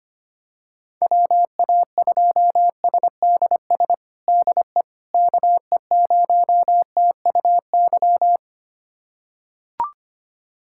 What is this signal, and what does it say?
Morse code, heard as a keyed tone, about 25 wpm, 700 Hz. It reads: WA2SDS DE KE0TUY
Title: USA Callsign DE Callsign - 6 Characters - 25wpm